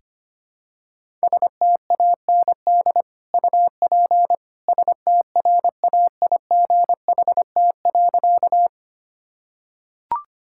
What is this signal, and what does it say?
Morse code, heard as a keyed tone, about 25 wpm, 700 Hz. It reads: STAND UP STRAIGHT.
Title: Sentences from Top 1000 Words - Review Part 6 - 25wpm